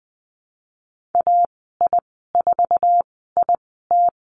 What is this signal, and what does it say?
Morse code, heard as a keyed tone, about 20 wpm, spaced as for 14 wpm, 700 Hz. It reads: AI4IT
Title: 069 Lesson 12g - Letter d - Call Signs 20wpm